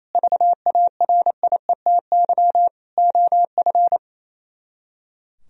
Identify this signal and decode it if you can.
Morse code, heard as a keyed tone, about 37 wpm, 700 Hz. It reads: VARIETY OF